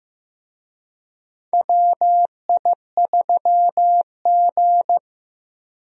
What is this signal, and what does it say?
Morse code, heard as a keyed tone, about 15 wpm, 700 Hz. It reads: WI3G
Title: USA Call Signs - 4 Characters - Repeated 3x - 15wpm